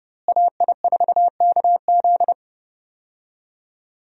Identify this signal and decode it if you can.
Morse code, heard as a keyed tone, about 32 wpm, 700 Hz. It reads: AI4KZ
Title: USA Call Signs - 5 Characters - Repeated 3x - 30wpm